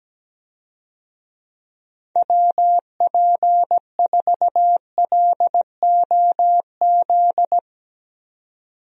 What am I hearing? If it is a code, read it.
Morse code, heard as a keyed tone, about 17 wpm, 700 Hz. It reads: WP4LOZ
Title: USA Call Signs - 6 Characters - 17wpm